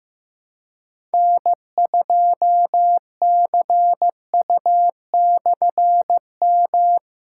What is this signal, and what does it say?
Morse code, heard as a keyed tone, about 15 wpm, 700 Hz. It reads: N2CU/M
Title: K1USN SST QSO Practice - 15wpm